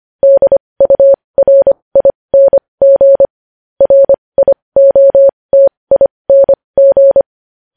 Morse code, heard as a keyed tone, about 25 wpm, spaced as for 20 wpm, 550 Hz. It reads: DURING RIOTING
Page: news-headlines-in-morse-code-at-20-wpm